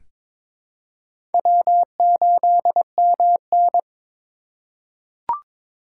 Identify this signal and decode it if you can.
Morse code, heard as a keyed tone, about 21 wpm, 700 Hz. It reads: W8MN
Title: USA Call Signs - 4 Characters - 6x Character Spacing - 22wpm